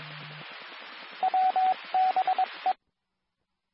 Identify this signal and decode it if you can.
Morse code, heard as a keyed tone, about 22 wpm, 750 Hz, with radio noise. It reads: WBE
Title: Nassau and Suffolk County Amateur Repeaters